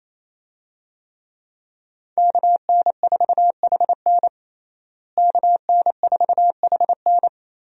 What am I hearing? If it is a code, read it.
Morse code, heard as a keyed tone, about 28 wpm, 700 Hz. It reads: KN4HN KN4HN
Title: USA Call Signs - 5 Characters - Repeated 3x - 28wpm